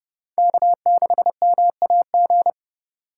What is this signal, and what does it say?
Morse code, heard as a keyed tone, about 29 wpm, 700 Hz. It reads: K6MAG